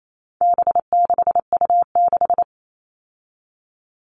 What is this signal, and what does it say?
Morse code, heard as a keyed tone, about 28 wpm, 700 Hz. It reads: B6U6
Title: All Cut Numbers - 4-Digit 28wpm